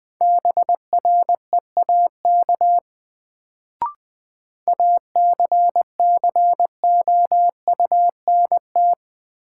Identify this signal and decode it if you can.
Morse code, heard as a keyed tone, about 20 wpm, 700 Hz. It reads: BREAK ACCOUNT
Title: Sentences from Top 900 Words - New Word Drill - 20wpm